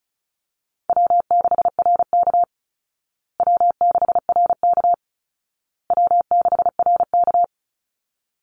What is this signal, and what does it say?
Morse code, heard as a keyed tone, about 34 wpm, 700 Hz. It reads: W6RK W6RK W6RK
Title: Call Signs Repeated 2x - 35wpm